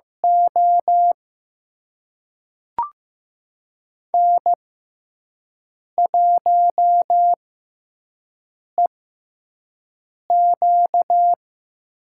Morse code, heard as a keyed tone, about 15 wpm, spaced as for 5 wpm, 700 Hz. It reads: O N1EQ
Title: USA Call Signs - 4 Characters - 6x Character Spacing - 15wpm